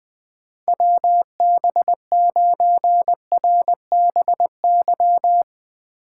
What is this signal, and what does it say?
Morse code, heard as a keyed tone, about 20 wpm, 700 Hz. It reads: WB9RBY
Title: USA Call Signs - 6 Characters - 2x Character Spacing - 20wpm